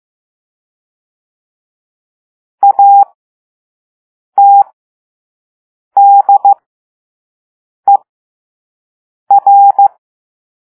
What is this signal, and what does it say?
Morse code, heard as a keyed tone, about 15 wpm, spaced as for 5 wpm, 800 Hz. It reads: ATDER
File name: Morse / Level3-All-B